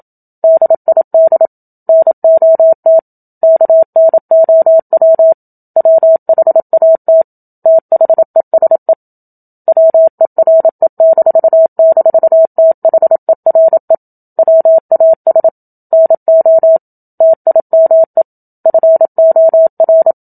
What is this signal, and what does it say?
Morse code, heard as a keyed tone, about 27 wpm, 650 Hz. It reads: DID NOT KNOW WHAT THESE WERE--THERE WAS NO TIME FOR
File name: Book Two / 01-Chapter 1 - 27 WPM